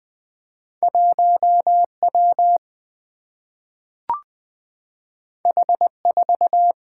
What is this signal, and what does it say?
Morse code, heard as a keyed tone, about 20 wpm, 700 Hz. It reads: 1W H4
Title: Two Letter-Number Combinations - 20wpm